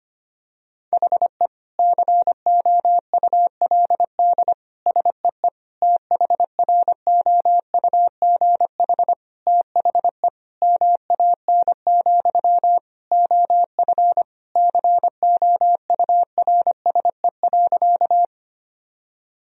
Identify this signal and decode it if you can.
Morse code, heard as a keyed tone, about 25 wpm, 700 Hz. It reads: HE COULD SEE THROUGH THE MAN, OF COURSE.
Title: Sentences from Top 1000 Words - Review Part 1 - 25wpm